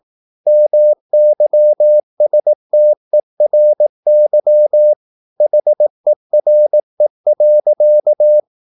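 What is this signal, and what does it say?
Morse code, heard as a keyed tone, about 18 wpm, 600 Hz. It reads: MYSTERY HERE.
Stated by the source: Morse practice at 18 wpm, English text, from CW med Gnister.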